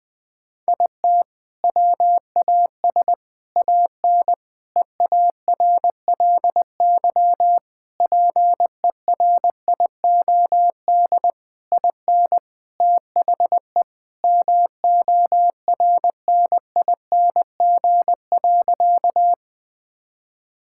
Morse code, heard as a keyed tone, about 20 wpm, 700 Hz. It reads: IT WAS AN EARLY PERIOD IN THE MORNING.
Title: Sentences from Top 1000 Words - Review Part 5 - 20wpm